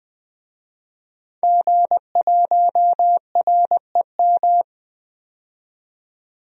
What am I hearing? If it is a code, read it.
Morse code, heard as a keyed tone, about 20 wpm, 700 Hz. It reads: G1REM
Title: DXCC-100 Call Signs - 5 Characters - 20wpm